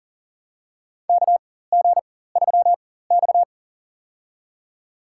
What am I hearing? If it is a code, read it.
Morse code, heard as a keyed tone, about 39 wpm, spaced as for 18 wpm, 700 Hz. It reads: KG3X
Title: USA Call Signs - 4x Character Spacing - 40wpm